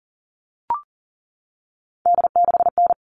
Morse code, heard as a keyed tone, about 40 wpm, 700 Hz. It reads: D6N
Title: All Cut Numbers - 3-Digit 40wpm